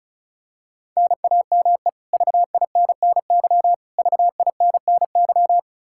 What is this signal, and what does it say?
Morse code, heard as a keyed tone, about 36 wpm, 700 Hz. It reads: NAME VINNY VINNY
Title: Practice QSO - Exchange 1 - Beginning - 35wpm